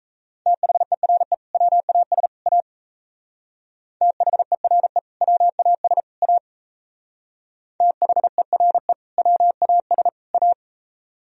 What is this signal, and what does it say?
Morse code, heard as a keyed tone, about 37 wpm, 700 Hz. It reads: THERE WAS A THERE WAS A THERE WAS A